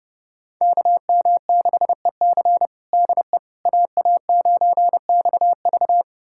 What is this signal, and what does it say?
Morse code, heard as a keyed tone, about 30 wpm, 700 Hz. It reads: KM6EC DE AA9XV
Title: USA Callsign DE Callsign - 5 Characters - 30wpm